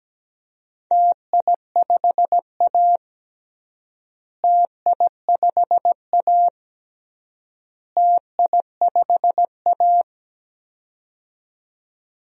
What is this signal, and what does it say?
Morse code, heard as a keyed tone, about 17 wpm, 700 Hz. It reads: TI5A TI5A TI5A
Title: DXCC-100 Call Signs - 4 Characters - Repeated 3x - 17wpm